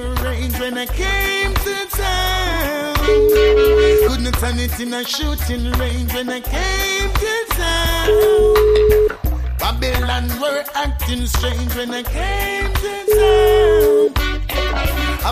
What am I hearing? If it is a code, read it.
Ringback tone (Europe, 425 Hz), with noise.